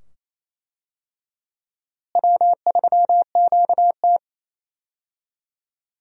Morse code, heard as a keyed tone, about 27 wpm, 700 Hz. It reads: W3QT
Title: USA Call Signs - 4 Characters - Repeated 2x - 28wpm